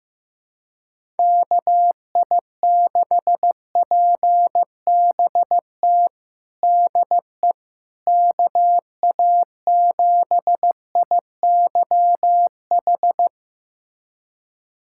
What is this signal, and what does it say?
Morse code, heard as a keyed tone, about 15 wpm, 700 Hz. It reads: KI6PBT DE KA7IYH
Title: USA Callsign DE Callsign - 6 Characters - 15wpm